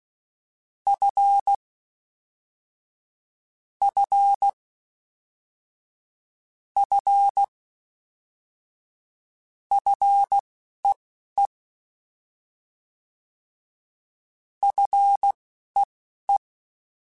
Morse code, heard as a keyed tone, about 16 wpm, spaced as for 11 wpm, 800 Hz. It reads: F F F FEE FEE
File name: Fs / Single F mp3s / Single F Prefixes 3x2 Twice @10